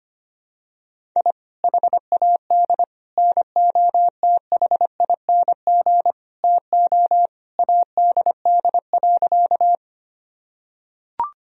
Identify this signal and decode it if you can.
Morse code, heard as a keyed tone, about 25 wpm, 700 Hz. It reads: I HAD NOTHING TO ADD.
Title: Sentences from Top 1000 Words - Review Part 1 - 25wpm